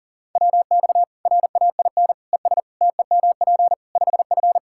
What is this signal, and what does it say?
Morse code, heard as a keyed tone, about 40 wpm, 700 Hz. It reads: WX RAIN ES TEMP 5F
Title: K1USN SST QSO Practice 40wpm